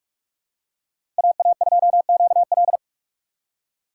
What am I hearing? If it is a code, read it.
Morse code, heard as a keyed tone, about 43 wpm, 700 Hz. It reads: AA2QL
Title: USA Call Signs - 5 Characters - 2x Character Spacing - 45wpm